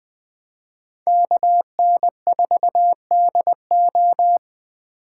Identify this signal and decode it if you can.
Morse code, heard as a keyed tone, about 23 wpm, 700 Hz. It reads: KN4DO
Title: Speed Racing - USA 2x2 Call Signs 20wpm